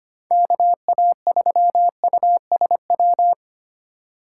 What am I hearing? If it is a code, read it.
Morse code, heard as a keyed tone, about 25 wpm, 700 Hz. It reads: KA3USW
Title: USA Call Signs - 6 Characters - 4x Character Spacing - 25wpm